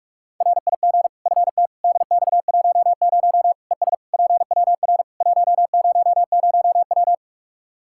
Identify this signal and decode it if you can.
Morse code, heard as a keyed tone, about 41 wpm, 700 Hz. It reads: RIG FT DX10 ES PWR 100W
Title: Practice QSO - Exchange 2 - Info - 45wpm